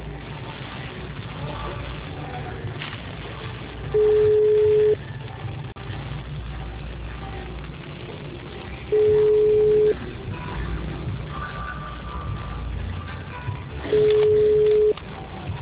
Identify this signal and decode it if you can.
Ringback tone (Europe, 425 Hz).